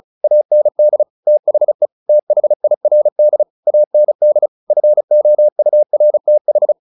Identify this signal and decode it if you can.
Morse code, heard as a keyed tone, about 34 wpm, 600 Hz. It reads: AND THE THIRD AND FOURTH
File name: the old man and the sea / the old man and the sea 1 35wpm